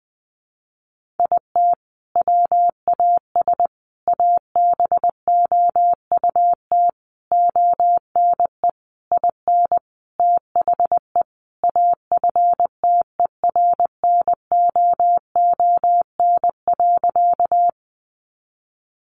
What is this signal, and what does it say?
Morse code, heard as a keyed tone, about 20 wpm, 700 Hz. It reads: IT WAS ABOUT ONE IN THE AFTERNOON.
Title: Sentences from Top 1500 Words 20wpm